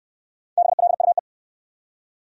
Morse code, heard as a keyed tone, about 65 wpm, 700 Hz. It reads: BLUE